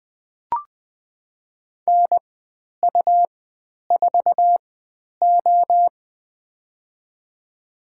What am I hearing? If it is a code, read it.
Morse code, heard as a keyed tone, about 20 wpm, spaced as for 10 wpm, 700 Hz. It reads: NU4O
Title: 102 Lesson 17g - Letter u - Call Signs 20wpm